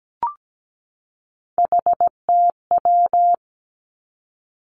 Morse code, heard as a keyed tone, about 17 wpm, 700 Hz. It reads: HTW
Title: Three Challenging Letter-Number Combinations 17wpm